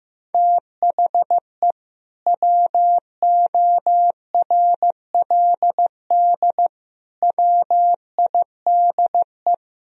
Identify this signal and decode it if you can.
Morse code, heard as a keyed tone, about 15 wpm, 700 Hz. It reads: THE WORLD WIDE
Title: Sentences from Top 1000 Words - Review Part 4 - 15wpm